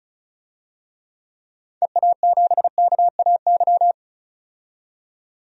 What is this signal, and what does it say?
Morse code, heard as a keyed tone, about 36 wpm, 700 Hz. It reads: EA7KAY
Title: DXCC-100 Call Signs - 6 Characters - 35wpm